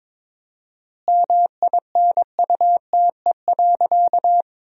Morse code, heard as a keyed tone, about 23 wpm, 700 Hz. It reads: MINUTE.
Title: 4x Word-Spacing - Sentences from Top 500 Words - 22wpm